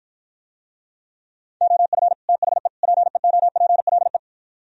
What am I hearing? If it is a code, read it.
Morse code, heard as a keyed tone, about 53 wpm, 700 Hz. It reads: OF THE PEOPLE